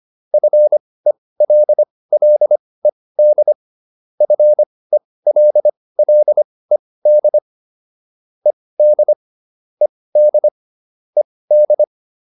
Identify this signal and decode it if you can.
Morse code, heard as a keyed tone, about 25 wpm, spaced as for 18 wpm, 600 Hz. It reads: FELLED FELLED ED ED ED